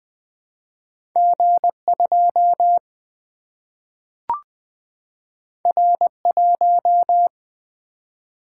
Morse code, heard as a keyed tone, about 20 wpm, 700 Hz. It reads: G 2 R 1
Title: Single-Letter Single-Number 20wpm